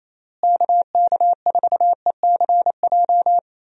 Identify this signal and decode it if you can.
Morse code, heard as a keyed tone, about 28 wpm, 700 Hz. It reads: KK4ECJ
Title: USA Call Signs - 6 Characters - Repeated 3x - 28wpm